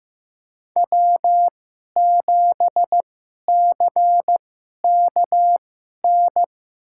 Morse code, heard as a keyed tone, about 15 wpm, spaced as for 11 wpm, 700 Hz. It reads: W7CKN
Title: USA Call Signs - 5 Characters - 2x Character Spacing - 15wpm